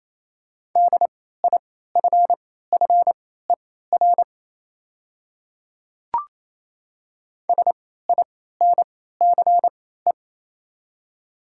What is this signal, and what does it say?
Morse code, heard as a keyed tone, about 27 wpm, spaced as for 16 wpm, 690 Hz. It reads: DIFFER SINCE
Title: 3x Character Spacing - Top 500 Words - 28wpm